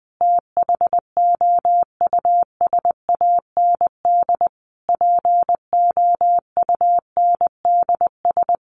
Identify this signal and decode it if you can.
Morse code, heard as a keyed tone, about 20 wpm, 700 Hz. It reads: THOUSAND POUNDS
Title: A Sound Of Thunder 20wpm